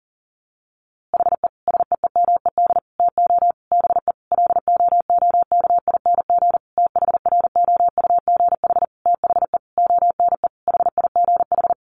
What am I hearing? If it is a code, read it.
Morse code, heard as a keyed tone, about 40 wpm, 700 Hz. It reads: HE SEEMED TO BE LOOKING THROUGH THE ONE HIGH
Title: A Sound Of Thunder 40wpm